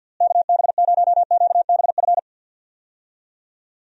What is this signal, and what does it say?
Morse code, heard as a keyed tone, about 47 wpm, 700 Hz. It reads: KB0QBF
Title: USA Call Signs - 6 Characters - Repeated 2x - 50wpm